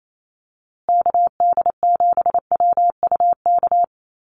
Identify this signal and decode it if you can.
Morse code, heard as a keyed tone, about 28 wpm, 700 Hz. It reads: KD7WUK
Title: USA Call Signs Repeated 3x 28wpm